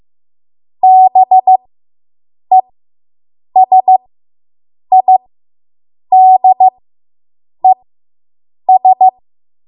Morse code, heard as a keyed tone, about 15 wpm, spaced as for 7 wpm, 750 Hz. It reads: BESIDES